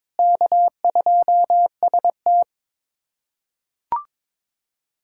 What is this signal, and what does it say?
Morse code, heard as a keyed tone, about 23 wpm, 700 Hz. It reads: K2ST
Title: USA Callsign DE Callsign - 4 Characters - 22wpm